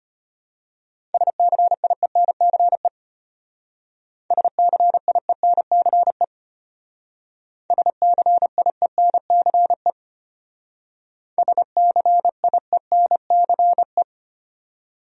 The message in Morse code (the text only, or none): SCIENCE SCIENCE SCIENCE SCIENCE